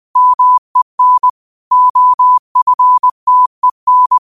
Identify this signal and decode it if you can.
Morse code, heard as a keyed tone, about 20 wpm, 1000 Hz. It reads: MEN OFTEN
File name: qotdfiles / 20wpm